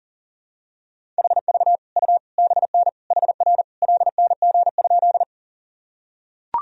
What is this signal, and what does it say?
Morse code, heard as a keyed tone, about 37 wpm, 700 Hz. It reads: HV U BN HR LNG?
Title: Sentences from Top 200 Words and Abbreviations 40wpm